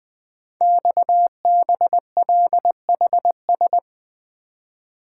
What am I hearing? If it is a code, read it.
Morse code, heard as a keyed tone, about 20 wpm, 700 Hz. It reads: XBLHS
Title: Code Groups - 5 Letters - 20wpm